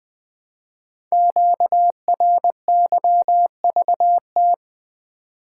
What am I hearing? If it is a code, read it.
Morse code, heard as a keyed tone, about 20 wpm, 700 Hz. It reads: QRYVT
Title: Code Groups - 5 Letter:Numbers - 20wpm